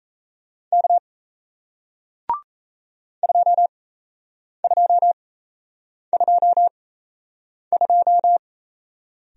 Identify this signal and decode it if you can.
Morse code, heard as a keyed tone, about 38 wpm, spaced as for 8 wpm, 700 Hz. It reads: K 2222